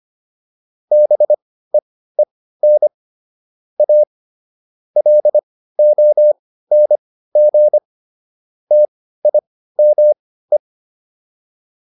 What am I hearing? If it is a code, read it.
Morse code, heard as a keyed tone, about 25 wpm, spaced as for 15 wpm, 600 Hz. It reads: BEEN A LONG TIME
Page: intermediate-practice-files